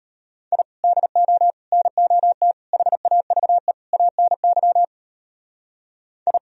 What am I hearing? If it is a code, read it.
Morse code, heard as a keyed tone, about 32 wpm, 700 Hz. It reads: I DO NOT HAVE ANY I